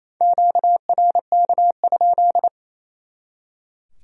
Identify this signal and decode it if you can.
Morse code, heard as a keyed tone, about 34 wpm, 700 Hz. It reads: QRK?
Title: Speed-Racing Q-Codes - 28wpm